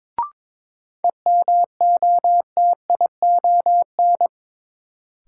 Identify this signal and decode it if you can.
Morse code, heard as a keyed tone, about 22 wpm, 700 Hz. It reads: EMOTION
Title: Sentences from Top 1600 Words - New Word Drill Alt - 22wpm